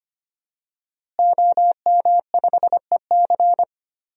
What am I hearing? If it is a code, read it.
Morse code, heard as a keyed tone, about 25 wpm, 700 Hz. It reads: OM5EC
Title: DXCC-100 Call Signs - 5 Characters - Repeated 3x - 25wpm